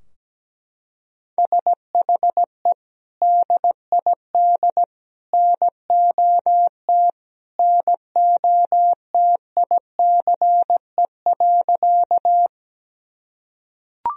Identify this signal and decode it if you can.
Morse code, heard as a keyed tone, about 17 wpm, 700 Hz. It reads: SHE DID NOT NOTICE.
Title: Sentences from Top 1000 Words - Review Part 2 - 17wpm